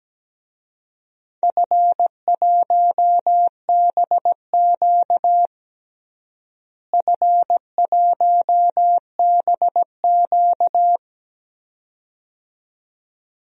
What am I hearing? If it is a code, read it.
Morse code, heard as a keyed tone, about 17 wpm, 700 Hz. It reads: F1BQ F1BQ
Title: DXCC-100 Call Signs - 4 Characters - Repeated 2x - 17wpm